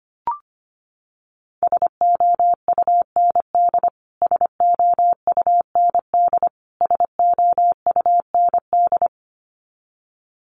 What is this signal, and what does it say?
Morse code, heard as a keyed tone, about 25 wpm, 700 Hz. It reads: SOUND SOUND SOUND
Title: Top 500 Words Repeated 25wpm